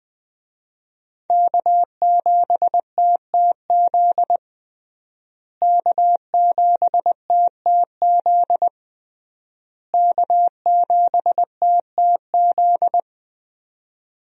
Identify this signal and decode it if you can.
Morse code, heard as a keyed tone, about 21 wpm, 700 Hz. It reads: K7TTZ K7TTZ K7TTZ
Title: USA Call Signs - 5 Characters - Repeated 3x - 20wpm